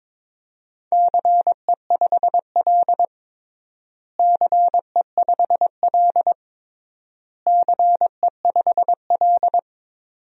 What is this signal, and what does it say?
Morse code, heard as a keyed tone, about 22 wpm, 700 Hz. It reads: CE5L CE5L CE5L
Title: DXCC-100 Call Signs - 4 Characters - Repeated 3x - 22wpm